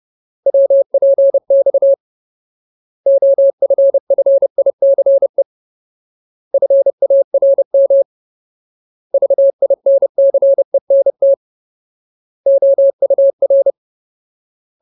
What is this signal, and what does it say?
Morse code, heard as a keyed tone, about 30 wpm, 550 Hz. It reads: WPX OFFICE FARM VINCENT OUR
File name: Ultimate Word List / 30WPM Ultimate